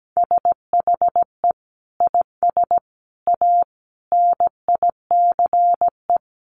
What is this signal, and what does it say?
Morse code, heard as a keyed tone, about 17 wpm, 700 Hz. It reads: SHE IS A NICE
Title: Sentences from Top 1100 Words 17wpm